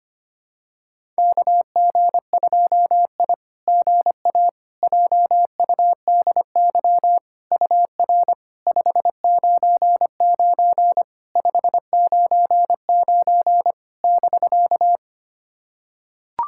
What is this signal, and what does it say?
Morse code, heard as a keyed tone, about 25 wpm, 690 Hz. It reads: KG2I GA JUDY UR 599 599 <BK>
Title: SOTA QSO Practice 25wpm